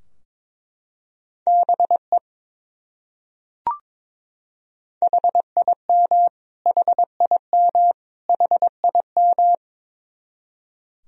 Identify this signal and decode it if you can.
Morse code, heard as a keyed tone, about 21 wpm, 700 Hz. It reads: BE HIM HIM HIM
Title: Top 100 Words Repeated 22wpm